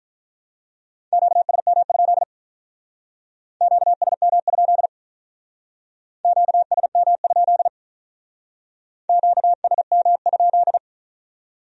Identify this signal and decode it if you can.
Morse code, heard as a keyed tone, about 43 wpm, 700 Hz. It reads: QSM? QSM? QSM? QSM?
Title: Speed-Racing Q-Codes - 35wpm